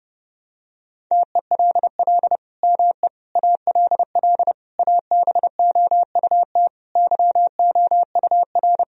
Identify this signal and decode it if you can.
Morse code, heard as a keyed tone, about 30 wpm, 700 Hz. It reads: TELL ME ALL ABOUT YOUR
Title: Sentences from Top 1000 Words - Review Part 4 - 30wpm